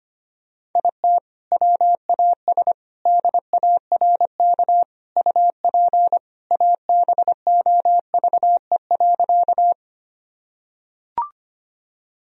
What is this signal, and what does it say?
Morse code, heard as a keyed tone, about 25 wpm, 700 Hz. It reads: IT WAS DARK UP ABOVE.
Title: Sentences from Top 1000 Words - Review Part 2 - 25wpm